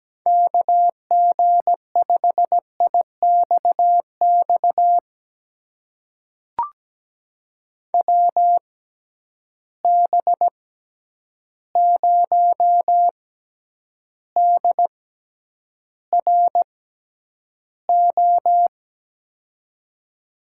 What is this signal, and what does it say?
Morse code, heard as a keyed tone, about 17 wpm, spaced as for 9 wpm, 700 Hz. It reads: KG5IXX WB0DRO
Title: USA Call Signs - 6 Characters - 6x Character Spacing - 17wpm